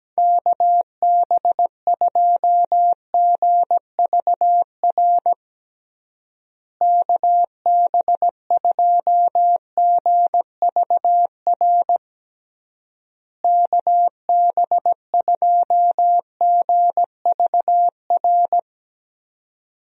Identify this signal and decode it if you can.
Morse code, heard as a keyed tone, about 17 wpm, 700 Hz. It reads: KB2GVR KB2GVR KB2GVR
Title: USA Call Signs - 6 Characters - Repeated 3x - 17wpm